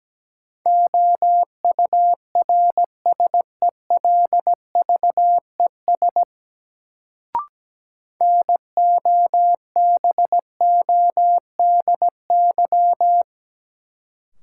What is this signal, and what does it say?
Morse code, heard as a keyed tone, about 17 wpm, 700 Hz. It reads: OURSELVES NOBODY
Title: Sentences from Top 1600 Words - New Word Drill Alt - 17wpm